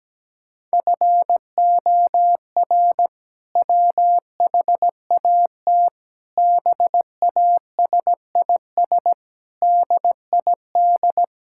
Morse code, heard as a keyed tone, about 17 wpm, 700 Hz. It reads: FOR WHAT BASIS DID
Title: Sentences from Top 1000 Words - Review Part 5 - 17wpm